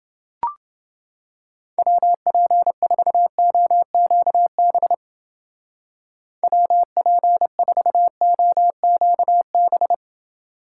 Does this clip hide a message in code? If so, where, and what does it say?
Yes: Morse code from 1.8 to 9.9 s, heard as a keyed tone, about 29 wpm, 700 Hz. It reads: WP4OQB WP4OQB